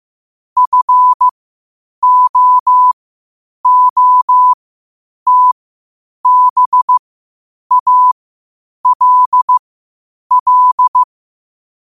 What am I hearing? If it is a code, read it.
Morse code, heard as a keyed tone, about 15 wpm, spaced as for 8 wpm, 1000 Hz. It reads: FOOTBALL
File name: qotdfiles / 8wpm